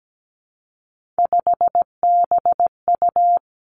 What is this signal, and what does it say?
Morse code, heard as a keyed tone, about 17 wpm, 700 Hz. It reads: 5BU
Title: Three Challenging Letter-Number Combinations 17wpm